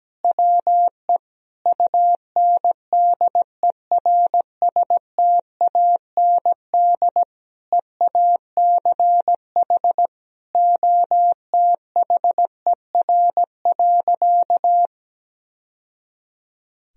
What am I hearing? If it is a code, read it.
Morse code, heard as a keyed tone, about 17 wpm, 700 Hz. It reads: WE UNDERSTAND EACH OTHER.
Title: Sentences from Top 1000 Words - Review Part 1 - 17wpm